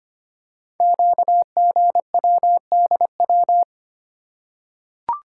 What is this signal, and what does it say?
Morse code, heard as a keyed tone, about 25 wpm, 700 Hz. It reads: QGWDW
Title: Code Groups - 5 Letters - 25wpm